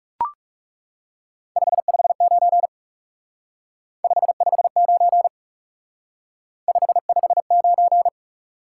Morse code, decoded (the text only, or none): 559 559 559